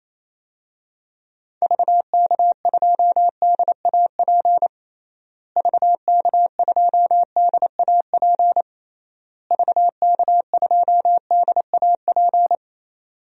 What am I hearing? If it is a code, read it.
Morse code, heard as a keyed tone, about 28 wpm, 700 Hz. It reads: VK2DAP VK2DAP VK2DAP
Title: DXCC-100 Call Signs - 6 Characters - Repeated 3x - 28wpm